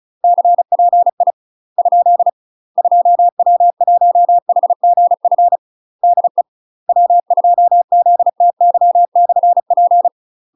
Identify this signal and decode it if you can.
Morse code, heard as a keyed tone, about 36 wpm, 700 Hz. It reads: CPI ? 2W1HGF DE W2ZTY/P
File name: UK MASTER QSOs TRAINING / 35WPM / UK MASTER QSOs 0002